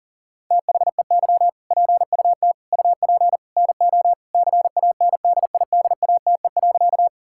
Morse code, heard as a keyed tone, about 40 wpm, 700 Hz. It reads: THEY PUT UP NO CANDIDATE.
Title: Sentences from Top 1600 Words 40wpm